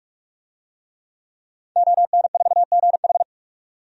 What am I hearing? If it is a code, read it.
Morse code, heard as a keyed tone, about 46 wpm, 700 Hz. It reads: ON4GH
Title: DXCC-100 Call Signs - 5 Characters - 45wpm